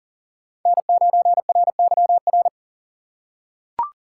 Morse code, heard as a keyed tone, about 40 wpm, 700 Hz. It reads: N9RYR
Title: USA Call Signs - 5 Characters - 4x Character Spacing - 40wpm